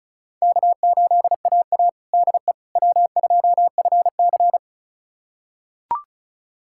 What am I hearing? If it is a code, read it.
Morse code, heard as a keyed tone, about 36 wpm, 700 Hz. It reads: K8AA DE W2FC
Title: USA Callsign DE Callsign - 4 Characters - 35wpm